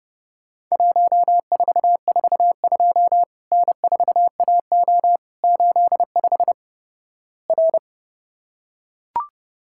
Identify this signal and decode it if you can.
Morse code, heard as a keyed tone, about 29 wpm, 700 Hz. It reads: 1442 N4AO 85 R
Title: Worked All Europe DX Contest - QTC Exchange - 30wpm